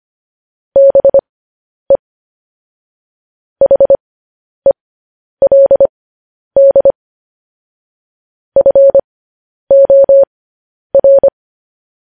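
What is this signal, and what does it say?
Morse code, heard as a keyed tone, about 25 wpm, spaced as for 10 wpm, 550 Hz. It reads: BE HELD FOR
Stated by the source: Morse code practice of news headlines at 10 wpm.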